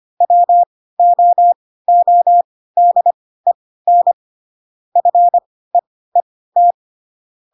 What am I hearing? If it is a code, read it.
Morse code, heard as a keyed tone, about 25 wpm, spaced as for 16 wpm, 700 Hz. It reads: WOODEN FEET